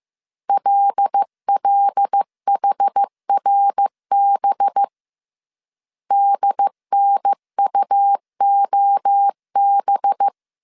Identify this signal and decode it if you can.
Morse code, heard as a keyed tone, about 15 wpm, 800 Hz. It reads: LLHRB DNUOB